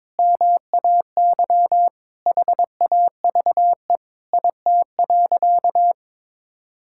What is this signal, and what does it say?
Morse code, heard as a keyed tone, about 21 wpm, 700 Hz. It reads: MAY HAVE IT.
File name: practice / 22wpm / Sentences with 2-Word Phrases - 22wpm